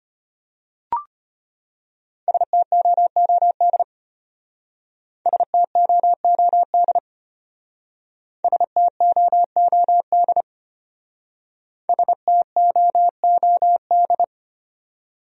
Morse code, heard as a keyed tone, about 33 wpm, 700 Hz. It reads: STOOD STOOD STOOD STOOD